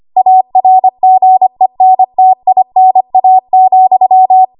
Morse code, heard as a keyed tone, about 25 wpm, 750 Hz. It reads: ARGENTINA,
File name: Morse / 25WPM